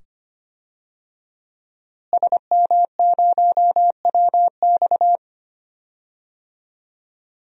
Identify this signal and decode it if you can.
Morse code, heard as a keyed tone, about 25 wpm, 700 Hz. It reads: SM0WX
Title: DXCC-100 Call Signs - 5 Characters - Repeated 2x - 25wpm